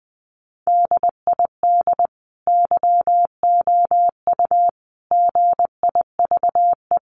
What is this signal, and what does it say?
Morse code, heard as a keyed tone, about 20 wpm, 700 Hz. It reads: DID YOU GIVE
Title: Sentences from Top 1500 Words 20wpm